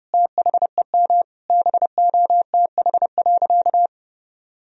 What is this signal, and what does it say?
Morse code, heard as a keyed tone, about 30 wpm, 700 Hz. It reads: THEM BOTH.
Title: Sentences from Top 1000 Words - Review Part 1 - 30wpm